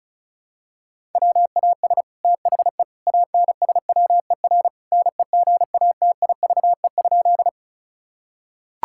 Morse code, heard as a keyed tone, about 36 wpm, 700 Hz. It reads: WAS THE ANSWER NEGATIVE?
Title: Sentences from Top 1000 Words - Review Part 6 - 35wpm